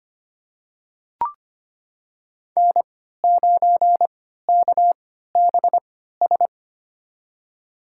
Morse code, heard as a keyed tone, about 25 wpm, spaced as for 14 wpm, 700 Hz. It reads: N9KBS
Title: USA Call Signs - 5 Characters - 3x Character Spacing - 25wpm